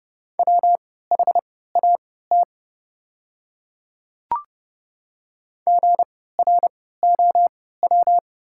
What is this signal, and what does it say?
Morse code, heard as a keyed tone, about 31 wpm, spaced as for 17 wpm, 700 Hz. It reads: WHAT GROW
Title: 3x Character Spacing - Top 500 Words - 30wpm